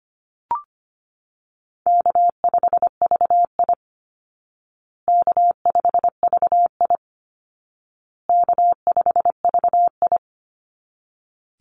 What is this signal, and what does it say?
Morse code, heard as a keyed tone, about 25 wpm, 700 Hz. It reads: K5VI K5VI K5VI
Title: Call Signs Repeated 2x - 25wpm